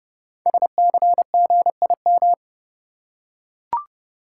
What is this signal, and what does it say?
Morse code, heard as a keyed tone, about 30 wpm, 700 Hz. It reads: SCGIM
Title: Code Groups - 5 Letter:Numbers - 30wpm